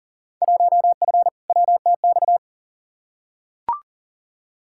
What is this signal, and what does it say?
Morse code, heard as a keyed tone, about 40 wpm, 700 Hz. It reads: 1F WTX
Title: Field Day Exchanges Encore - 40wpm